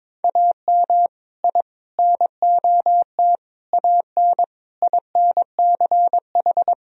Morse code, heard as a keyed tone, about 21 wpm, 700 Hz. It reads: AM I NOT AN INCH
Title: Sentences from Top 1000 Words - Review Part 3 - 22wpm